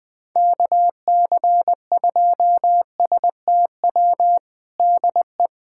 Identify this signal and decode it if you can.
Morse code, heard as a keyed tone, about 20 wpm, 700 Hz. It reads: KC2STW DE
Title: USA Callsign DE Callsign - 6 Characters - 20wpm